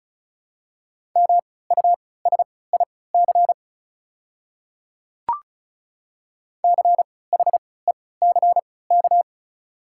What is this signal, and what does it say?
Morse code, heard as a keyed tone, about 36 wpm, spaced as for 20 wpm, 690 Hz. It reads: MUSIC CHECK
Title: 3x Character Spacing - Top 500 Words - 35wpm